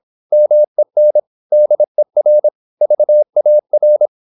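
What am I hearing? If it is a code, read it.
Morse code, heard as a keyed tone, about 26 wpm, 600 Hz. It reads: MEN DER VAR
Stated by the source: High-speed (QRQ) Morse at 26 wpm.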